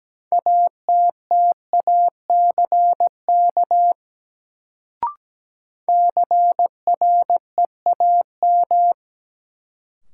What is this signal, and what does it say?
Morse code, heard as a keyed tone, about 17 wpm, 700 Hz. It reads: ATTACK CREAM
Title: Sentences from Top 1500 Words - New Word Drill - 17wpm